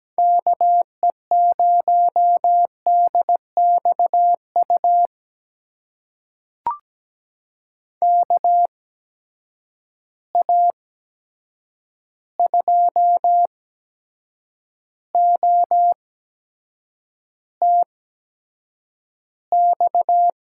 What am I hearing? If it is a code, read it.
Morse code, heard as a keyed tone, about 17 wpm, spaced as for 7 wpm, 700 Hz. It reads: KE0DXU KA2OTX
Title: USA Call Signs - 6 Characters - 8x Character Spacing - 17wpm